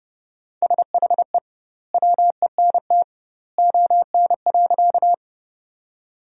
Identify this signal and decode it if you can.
Morse code, heard as a keyed tone, about 30 wpm, 700 Hz. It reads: SHE WENT ON.
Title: 2x Word-Spacing - Sentences from Top 200 Words - 30wpm